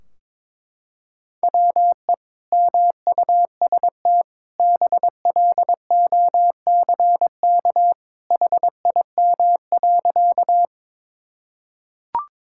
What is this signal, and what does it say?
Morse code, heard as a keyed tone, about 22 wpm, 700 Hz. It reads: WE MUST BLOCK HIM.
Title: Sentences from Top 1600 Words 22wpm